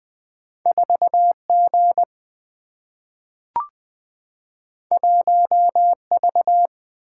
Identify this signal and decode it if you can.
Morse code, heard as a keyed tone, about 20 wpm, 700 Hz. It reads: 4G 1V
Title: Single-Letter Single-Number 20wpm